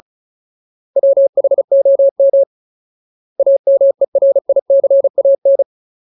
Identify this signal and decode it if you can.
Morse code, heard as a keyed tone, about 34 wpm, 550 Hz. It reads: WHOM AMERICAN
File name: Ultimate Word List / 35WPM Ultimate